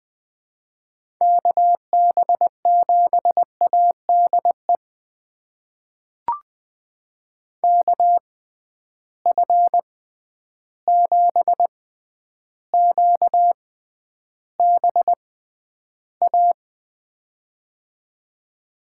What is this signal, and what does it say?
Morse code, heard as a keyed tone, about 20 wpm, spaced as for 10 wpm, 700 Hz. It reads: KB7ADE KF7QBA
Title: USA Call Signs - 6 Characters - 6x Character Spacing - 20wpm